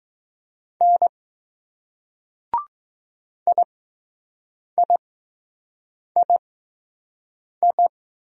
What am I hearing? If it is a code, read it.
Morse code, heard as a keyed tone, about 20 wpm, spaced as for 6 wpm, 700 Hz. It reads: N IIII